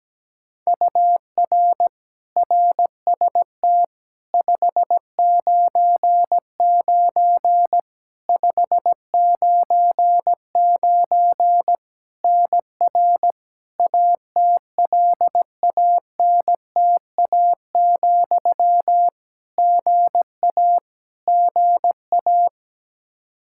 Morse code, heard as a keyed tone, about 17 wpm, 700 Hz. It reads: UR RST 599 599 NR ATLANTA, GA GA
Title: Practice QSO - Exchange 1 - Beginning - 17wpm